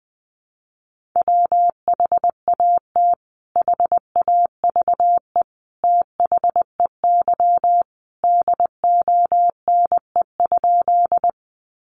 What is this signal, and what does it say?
Morse code, heard as a keyed tone, about 20 wpm, 700 Hz. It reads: WHAT HAVE THEY DONE?
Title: Sentences from Top 500 Words 20wpm